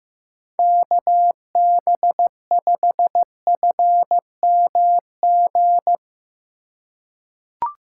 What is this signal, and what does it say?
Morse code, heard as a keyed tone, about 15 wpm, 700 Hz. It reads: KB5FMG
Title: USA Call Signs - 6 Characters - 4x Character Spacing - 15wpm